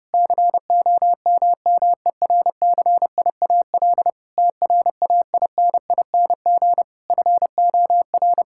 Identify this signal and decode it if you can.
Morse code, heard as a keyed tone, about 30 wpm, 700 Hz. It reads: COMMERCIAL TRAINING FOR
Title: Sentences from Top 1000 Words - Review Part 5 - 30wpm